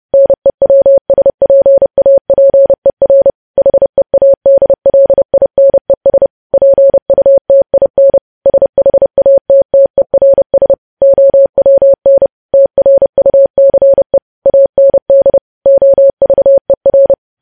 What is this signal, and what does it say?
Morse code, heard as a keyed tone, about 30 wpm, 550 Hz. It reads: NEWSPAPER HEADLINES PUTIN SHATTERS OWN TRUCE AND OVER
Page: news headlines in morse code at 30 wpm